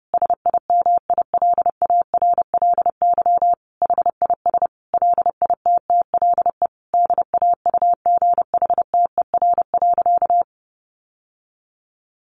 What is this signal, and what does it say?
Morse code, heard as a keyed tone, about 30 wpm, 700 Hz. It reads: SIMILARLY HIS LITTLE DAUGHTER.
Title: Sentences from Top 1500 Words 30wpm